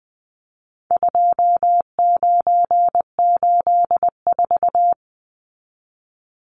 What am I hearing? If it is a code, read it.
Morse code, heard as a keyed tone, about 20 wpm, 700 Hz. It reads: 2984
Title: Numbers - 4-Digit 20wpm